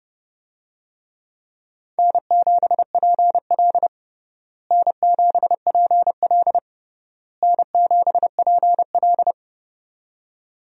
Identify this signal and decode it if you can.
Morse code, heard as a keyed tone, about 30 wpm, 700 Hz. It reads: N7PL N7PL N7PL
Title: USA Call Signs - 4 Characters - Repeated 3x - 30wpm